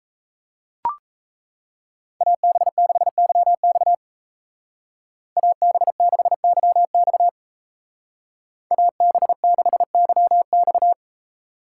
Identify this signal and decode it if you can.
Morse code, heard as a keyed tone, about 34 wpm, 700 Hz. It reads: AB6YX AB6YX AB6YX
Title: Speed Racing - USA 2x2 Call Signs 28wpm